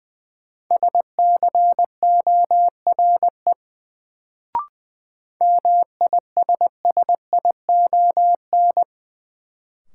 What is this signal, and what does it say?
Morse code, heard as a keyed tone, about 20 wpm, 700 Hz. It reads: SCORE MISSION
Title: Sentences from Top 1600 Words - New Word Drill - 20wpm